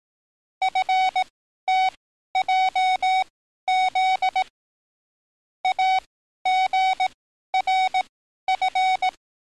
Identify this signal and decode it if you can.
Morse code, heard as a keyed tone, about 18 wpm, spaced as for 12 wpm, 750 Hz. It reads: FTJZ AGRF
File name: training / NAVPERS 11049to11060 / NAVPERS 11053 RA RB